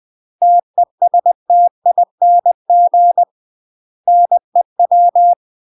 Morse code, heard as a keyed tone, about 20 wpm, 700 Hz. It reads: TESTING NEW
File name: UK MASTER QSOs TRAINING / 20WPM / UK MASTER QSOs 0022